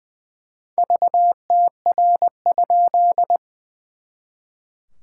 Morse code, heard as a keyed tone, about 20 wpm, 700 Hz. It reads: VTR?
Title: SOTA QSO Practice 20wpm